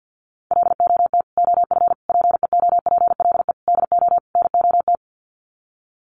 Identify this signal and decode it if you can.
Morse code, heard as a keyed tone, about 50 wpm, 700 Hz. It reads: LOT OF PEOPLE DO NOT
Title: Sets of 5 Words Encore 50wpm